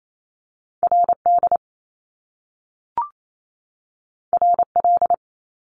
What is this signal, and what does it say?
Morse code, heard as a keyed tone, about 28 wpm, 700 Hz. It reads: RD RL
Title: Top Two Letter Combinations in Words 28wpm